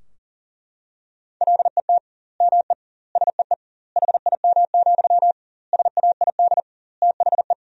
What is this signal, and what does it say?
Morse code, heard as a keyed tone, about 40 wpm, 700 Hz. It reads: LET ME SEE HIM, SAID THE
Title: 2x Word-Spacing - Sentences from Top 400 Words - 40wpm